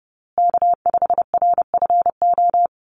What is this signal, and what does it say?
Morse code, heard as a keyed tone, about 30 wpm, 700 Hz. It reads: K5RFO
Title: USA Call Signs Repeated 2x 30wpm